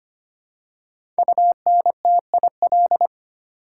Morse code, heard as a keyed tone, about 25 wpm, 700 Hz. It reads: UNTIL
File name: 5 Letter Words - First 3 Letters Given - 25wpm